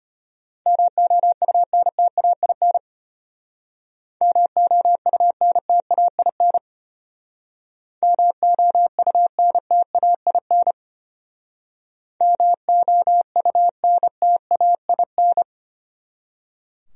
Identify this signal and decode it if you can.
Morse code, heard as a keyed tone, about 33 wpm, 700 Hz. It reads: MOUNTAIN MOUNTAIN MOUNTAIN MOUNTAIN